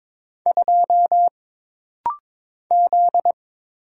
Morse code, heard as a keyed tone, about 22 wpm, 700 Hz. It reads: 2 Z
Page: This rapid-fire timing is geared to push your ICR (Instant Character Recognition) skills.